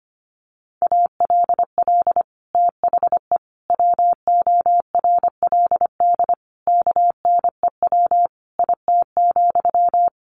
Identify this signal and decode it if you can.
Morse code, heard as a keyed tone, about 25 wpm, 700 Hz. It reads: ALL THE WORLD KNEW IT,
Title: Sentences from Top 1200 Words 25wpm